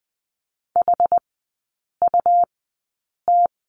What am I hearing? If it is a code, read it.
Morse code, heard as a keyed tone, about 20 wpm, spaced as for 8 wpm, 700 Hz. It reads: HUT
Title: 198 Lesson 32e - Letter k - Three Characters 20wpm